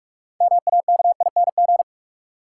Morse code, heard as a keyed tone, about 46 wpm, 700 Hz. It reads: MAKING
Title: Sentences from Top 1000 Words - Review Part 6 - New Word Drill - 45wpm